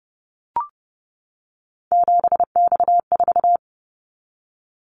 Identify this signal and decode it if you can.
Morse code, heard as a keyed tone, about 30 wpm, 700 Hz. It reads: 7X4
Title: Callsign Prefixes for Top 100 DXCC - 30wpm